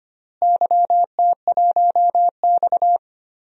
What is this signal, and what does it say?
Morse code, heard as a keyed tone, about 25 wpm, 700 Hz. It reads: YT1X
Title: DXCC-100 Call Signs - 4 Characters - Repeated 3x - 25wpm